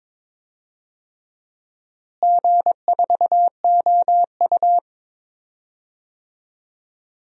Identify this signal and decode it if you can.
Morse code, heard as a keyed tone, about 22 wpm, 700 Hz. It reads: G4OU
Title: DXCC-100 Call Signs - 4 Characters - Repeated 3x - 22wpm